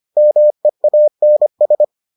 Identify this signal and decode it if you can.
Morse code, heard as a keyed tone, about 25 wpm, 600 Hz. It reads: MEANS